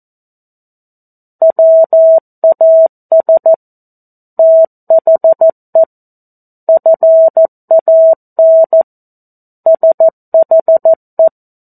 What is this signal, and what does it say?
Morse code, heard as a keyed tone, about 14 wpm, 650 Hz. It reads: WAS THE FAN SHE
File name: 02-Chapter-02 14 WPM